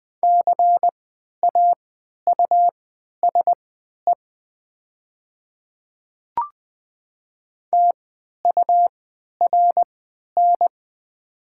Morse code, heard as a keyed tone, about 20 wpm, spaced as for 11 wpm, 700 Hz. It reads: CAUSE TURN
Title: 3x Character Spacing - Top 300 Words - 20wpm